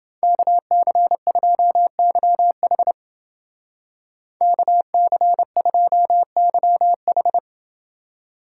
Morse code, heard as a keyed tone, about 28 wpm, 700 Hz. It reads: KC2YH KC2YH